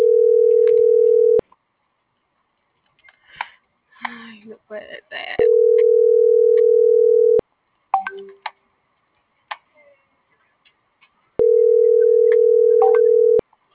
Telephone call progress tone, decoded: ringback tone

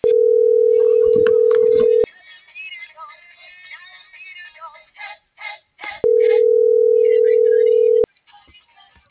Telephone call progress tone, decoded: ringback tone

